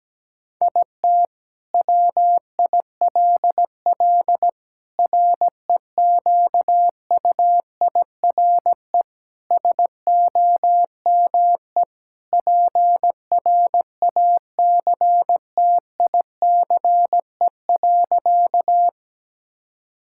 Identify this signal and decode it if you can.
Morse code, heard as a keyed tone, about 17 wpm, 700 Hz. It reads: IT WILL REQUIRE SOME PRACTICE.
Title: Sentences from Top 1000 Words - Review Part 6 - 17wpm